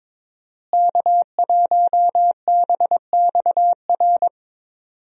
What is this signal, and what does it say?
Morse code, heard as a keyed tone, about 21 wpm, 700 Hz. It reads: K1BXR
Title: USA Call Signs - 5 Characters - Repeated 3x - 22wpm